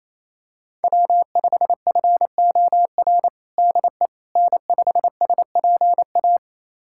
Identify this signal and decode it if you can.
Morse code, heard as a keyed tone, about 28 wpm, 700 Hz. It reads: W5FOR DE N5SPA
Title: USA Callsign DE Callsign - 5 Characters - 28wpm